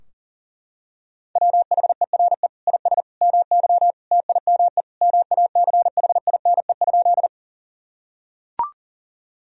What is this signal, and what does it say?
Morse code, heard as a keyed tone, about 40 wpm, 700 Hz. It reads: WHERE IS MY TIME MACHINE?
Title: Sentences from Top 1000 Words - Review Part 2 - 40wpm